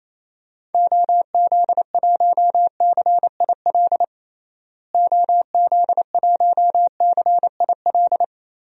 Morse code, heard as a keyed tone, about 28 wpm, 700 Hz. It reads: OZ1CIL OZ1CIL
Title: DXCC-100 Call Signs - 6 Characters - Repeated 2x - 28wpm